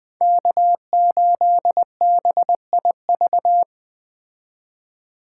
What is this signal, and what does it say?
Morse code, heard as a keyed tone, about 20 wpm, 700 Hz. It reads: K8BIV
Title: Speed Racing - USA 1x3 Call Signs 17wpm